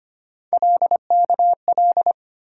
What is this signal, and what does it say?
Morse code, heard as a keyed tone, about 25 wpm, 700 Hz. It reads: LKL
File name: Three Letter-Number Combinations - 25wpm